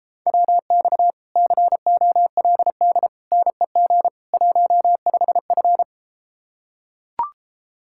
Morse code, heard as a keyed tone, about 33 wpm, 700 Hz. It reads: WX COLD NEG 15F